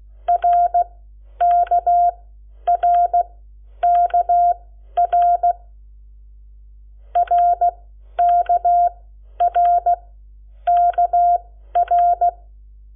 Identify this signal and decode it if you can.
Morse code, heard as a keyed tone, about 16 wpm, spaced as for 10 wpm, 700 Hz. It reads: RKRKR RKRKR
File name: Lecciones / CLASE Nº7